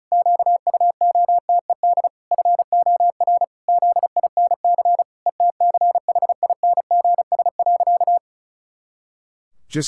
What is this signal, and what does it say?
Morse code, heard as a keyed tone, about 36 wpm, 700 Hz. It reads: QUOTED FOR ZINC ETCHINGS.
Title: Pangram Challenge v2 - 35wpm